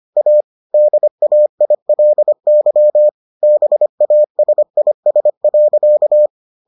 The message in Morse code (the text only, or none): A DAILY BASIS.